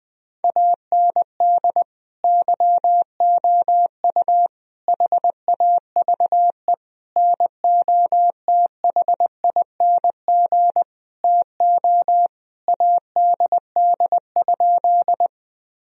Morse code, heard as a keyed tone, about 20 wpm, 700 Hz. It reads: AND YOU HAVE NOTHING TO ADD?
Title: Sentences from Top 1000 Words - Review Part 1 - 20wpm